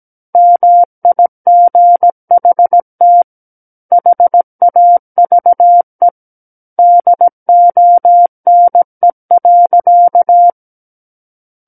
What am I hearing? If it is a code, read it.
Morse code, heard as a keyed tone, about 17 wpm, 700 Hz. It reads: MIGHT HAVE DONE.
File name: Book One / 08-Chapter 8 - 17 WPM